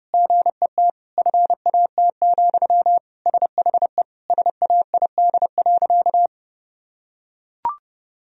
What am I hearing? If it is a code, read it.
Morse code, heard as a keyed tone, about 30 wpm, 700 Hz. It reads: GET FAT, SHE SAID.
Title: Sentences from Top 1000 Words - Review Part 4 - 30wpm